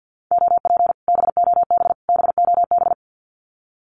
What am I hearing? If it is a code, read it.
Morse code, heard as a keyed tone, about 48 wpm, 700 Hz. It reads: OP BOB BOB
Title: Common QSO Phrases Encore 50wpm